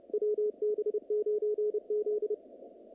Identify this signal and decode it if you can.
Morse code, heard as a keyed tone, about 31 wpm, 440 Hz, with radio noise. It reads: WB9Z
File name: contests / naqp-cw-k5tr-op-k5pi- -2200